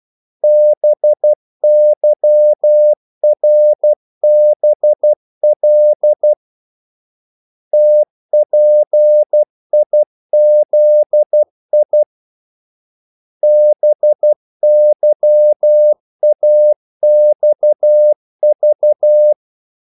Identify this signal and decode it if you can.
Morse code, heard as a keyed tone, about 12 wpm, 600 Hz. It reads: BYRBL TPIZI BYAXV
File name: code-exemple